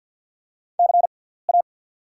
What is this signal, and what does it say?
Morse code, heard as a keyed tone, about 48 wpm, spaced as for 17 wpm, 690 Hz. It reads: CA
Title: 6x Character Spacing - Top 300 Words - 50wpm